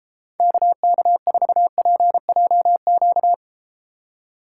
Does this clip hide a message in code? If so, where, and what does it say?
Yes: Morse code from 0.4 to 3.3 s, heard as a keyed tone, about 33 wpm, 700 Hz. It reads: KK4PJQ